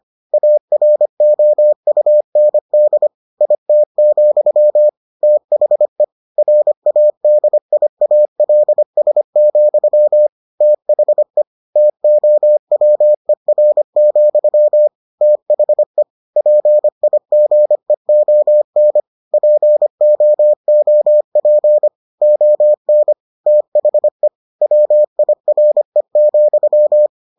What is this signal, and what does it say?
Morse code, heard as a keyed tone, about 25 wpm, 600 Hz. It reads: AROUND IT, THE RADIALS, THE TOWER, THE PIGEON POOP ON THE WIRE,